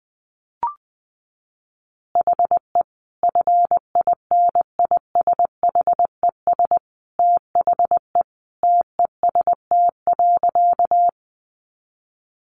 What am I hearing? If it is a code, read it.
Morse code, heard as a keyed tone, about 19 wpm, 700 Hz. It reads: HE FINISHES THE TEST.
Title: Additional Easy Phrases 20wpm